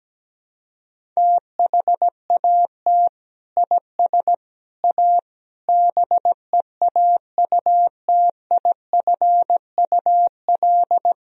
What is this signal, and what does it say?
Morse code, heard as a keyed tone, about 17 wpm, 700 Hz. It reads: THAT IS A BEAUTIFUL
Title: Sentences from Top 1000 Words - Review Part 2 - 17wpm